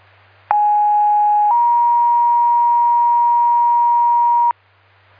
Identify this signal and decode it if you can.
Two-tone page, heard as a one-second tone, then a three-second tone, about 830 then 950 Hz. It is heard from 0.5 to 4.5 s.